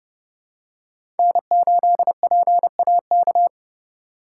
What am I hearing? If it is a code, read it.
Morse code, heard as a keyed tone, about 31 wpm, 700 Hz. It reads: N8PAK
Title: USA Call Signs - 4x Character Spacing - 30wpm